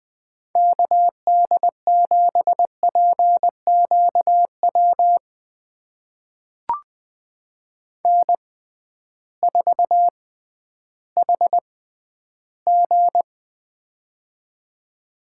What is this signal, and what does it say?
Morse code, heard as a keyed tone, about 20 wpm, spaced as for 12 wpm, 700 Hz. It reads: KD7PQW N4HG
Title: USA Call Signs - 6x Character Spacing - 20wpm